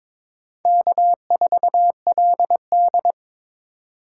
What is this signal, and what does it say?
Morse code, heard as a keyed tone, about 22 wpm, 700 Hz. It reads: K4LD
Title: USA Call Signs - 4 Characters - 8x Character Spacing - 22wpm